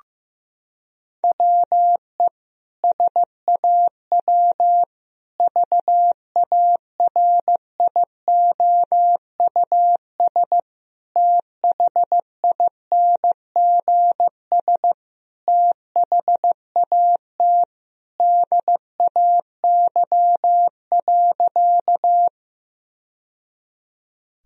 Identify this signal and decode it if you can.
Morse code, heard as a keyed tone, about 15 wpm, 700 Hz. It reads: WE SAW VARIOUS THINGS THAT DAY.
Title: Sentences from Top 1000 Words - Review Part 4 - 15wpm